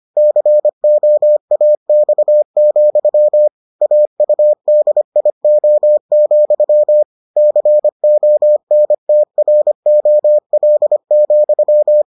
Morse code, heard as a keyed tone, about 25 wpm, 600 Hz. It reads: COAX, AUDIO, CONTROL,